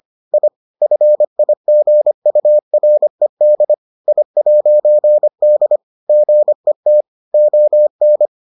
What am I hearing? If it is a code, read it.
Morse code, heard as a keyed tone, about 25 wpm, 600 Hz. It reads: I FIGURED I'D GET ON